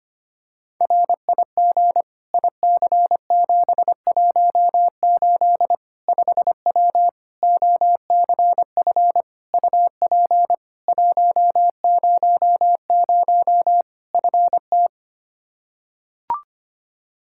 Morse code, heard as a keyed tone, about 25 wpm, 700 Hz. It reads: RIG IC718 5W OCF UP 100 FT
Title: K1USN SST QSO Practice 25wpm